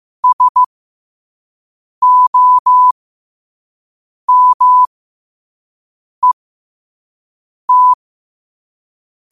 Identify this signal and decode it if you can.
Morse code, heard as a keyed tone, about 15 wpm, spaced as for 5 wpm, 1000 Hz. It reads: SOMET
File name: qotdfiles / 5wpm